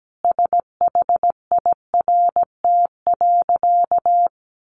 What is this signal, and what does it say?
Morse code, heard as a keyed tone, about 17 wpm, 700 Hz. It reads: SHIRT.
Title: A Sound Of Thunder 17wpm